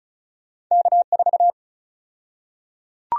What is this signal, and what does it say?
Morse code, heard as a keyed tone, about 36 wpm, 700 Hz. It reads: K4
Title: USA Call Sign Prefixes - 35wpm